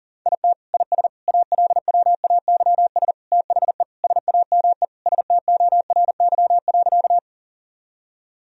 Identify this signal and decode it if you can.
Morse code, heard as a keyed tone, about 40 wpm, 700 Hz. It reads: IT IS ALWAYS THE SAME STORY.
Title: Sentences from Top 1000 Words - Review Part 1 - 40wpm